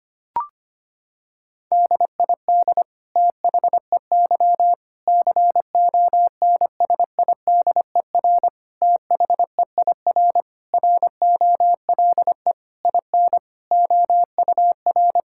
Morse code, heard as a keyed tone, about 25 wpm, 700 Hz. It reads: DID THEY CONSIDER THEIR ROLE IN OUR
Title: Sentences from Top 1000 Words - Review Part 6 - 25wpm